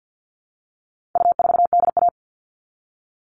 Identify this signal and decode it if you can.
Morse code, heard as a keyed tone, about 50 wpm, 700 Hz. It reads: U4DA
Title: All Cut Numbers - 4-Digit 50wpm